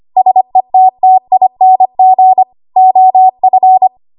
Morse code, heard as a keyed tone, about 25 wpm, 750 Hz. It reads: SETTING OF
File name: Archive / 25 WPM / 25WPM